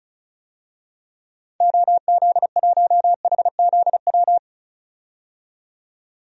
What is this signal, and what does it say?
Morse code, heard as a keyed tone, about 36 wpm, 700 Hz. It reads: OZ1HZW
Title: DXCC-100 Call Signs - 6 Characters - 35wpm